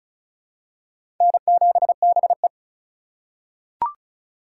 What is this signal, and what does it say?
Morse code, heard as a keyed tone, about 36 wpm, 700 Hz. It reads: N7BE
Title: USA Call Signs - 4 Characters - 2x Character Spacing - 35wpm